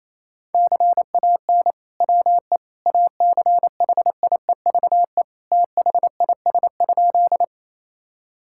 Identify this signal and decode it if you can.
Morse code, heard as a keyed tone, about 28 wpm, 700 Hz. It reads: CAN WE ACHIEVE THIS?
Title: Sentences from Top 1000 Words - Review Part 6 - 28wpm